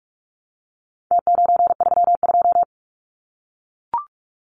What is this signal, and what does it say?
Morse code, heard as a keyed tone, about 44 wpm, 700 Hz. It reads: T932
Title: Numbers - Contest Style - 4-Digit 45wpm